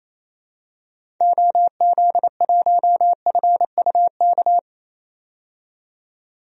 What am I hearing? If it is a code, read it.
Morse code, heard as a keyed tone, about 28 wpm, 700 Hz. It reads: OZ1FUK
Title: DXCC-100 Call Signs - 6 Characters - 28wpm